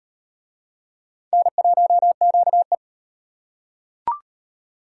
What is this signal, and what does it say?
Morse code, heard as a keyed tone, about 38 wpm, 700 Hz. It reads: N1QE